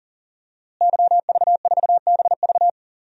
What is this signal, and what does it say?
Morse code, heard as a keyed tone, about 37 wpm, 700 Hz. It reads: YV4BV
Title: DXCC-100 Call Signs - 5 Characters - Repeated 3x - 40wpm